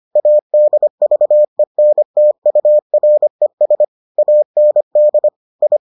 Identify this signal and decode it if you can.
Morse code, heard as a keyed tone, about 25 wpm, 600 Hz. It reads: ADVENTURES AND I